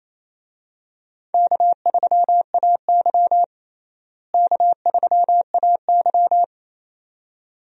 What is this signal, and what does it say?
Morse code, heard as a keyed tone, about 28 wpm, 700 Hz. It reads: K3AY K3AY
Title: USA Call Signs - 4 Characters - Repeated 2x - 28wpm